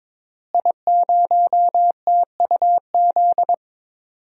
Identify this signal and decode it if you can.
Morse code, heard as a keyed tone, about 22 wpm, 700 Hz. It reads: I0TUZ
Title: DXCC-100 Call Signs - 5 Characters - Repeated 3x - 22wpm